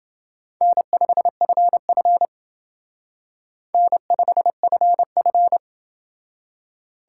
Morse code, heard as a keyed tone, about 28 wpm, 700 Hz. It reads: N5FF N5FF